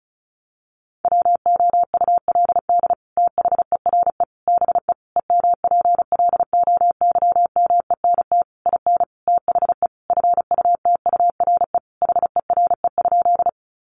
Morse code, heard as a keyed tone, about 34 wpm, 700 Hz. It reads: WOULD THERE BE EMPLOYMENT IN THE FUTURE HERE?
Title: Sentences from Top 1500 Words 35wpm